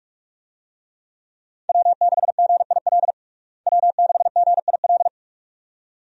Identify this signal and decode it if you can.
Morse code, heard as a keyed tone, about 45 wpm, 700 Hz. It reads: W6GIL W6GIL
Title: USA Call Signs - 5 Characters - Repeated 2x - 45wpm